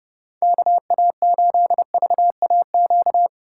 Morse code, heard as a keyed tone, about 37 wpm, 700 Hz. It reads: KA8VAQ
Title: Speed Racing - USA 2x3 Call Signs 30wpm